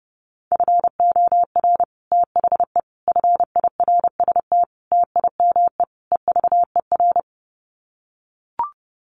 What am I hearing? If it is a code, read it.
Morse code, heard as a keyed tone, about 30 wpm, 700 Hz. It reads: FOR THE FIRST TIME EVER
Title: Sets of 5 Words Encore2 30wpm